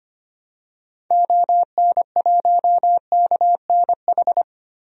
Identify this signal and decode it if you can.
Morse code, heard as a keyed tone, about 25 wpm, 690 Hz. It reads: ON1KNH
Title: DXCC-100 Call Signs - 6 Characters - 25wpm